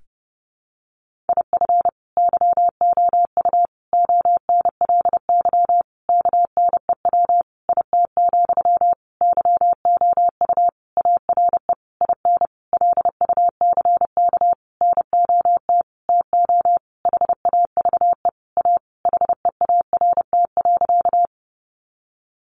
Morse code, heard as a keyed tone, about 30 wpm, 700 Hz. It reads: IF YOU ONLY KNEW IT, YOU ARE IN LUCK NOT TO HAVE A HEART.
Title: Sentences from Top 1500 Words 30wpm